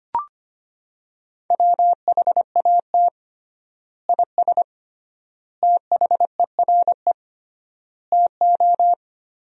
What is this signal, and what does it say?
Morse code, heard as a keyed tone, about 25 wpm, 700 Hz. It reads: WHAT IS THERE TO
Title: 3x Word-Spacing - Sentences from Top 400 Words - 25wpm